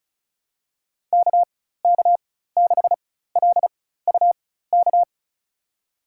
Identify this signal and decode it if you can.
Morse code, heard as a keyed tone, about 36 wpm, spaced as for 16 wpm, 700 Hz. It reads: KK6LUK
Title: USA Call Signs - 4x Character Spacing - 35wpm